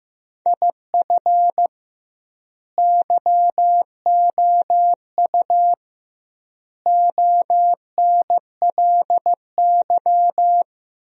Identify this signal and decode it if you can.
Morse code, heard as a keyed tone, about 15 wpm, 700 Hz. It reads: IF YOU ONLY
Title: 2x Word-Spacing - Sentences from Top 400 Words - 15wpm